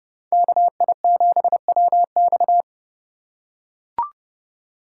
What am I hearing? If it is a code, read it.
Morse code, heard as a keyed tone, about 28 wpm, 700 Hz. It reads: KI7WX